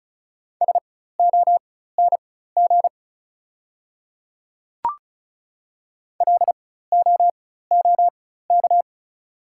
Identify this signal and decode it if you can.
Morse code, heard as a keyed tone, about 36 wpm, spaced as for 16 wpm, 690 Hz. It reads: SONG LOOK
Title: 4x Character Spacing - Top 500 Words - 35wpm